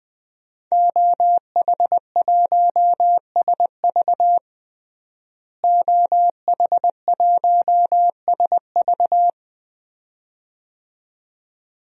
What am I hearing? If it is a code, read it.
Morse code, heard as a keyed tone, about 20 wpm, 700 Hz. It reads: OH1SV OH1SV
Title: DXCC-100 Call Signs - 5 Characters - Repeated 2x - 20wpm